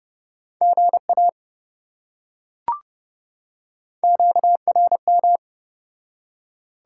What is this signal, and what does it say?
Morse code, heard as a keyed tone, about 27 wpm, 700 Hz. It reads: GA QRM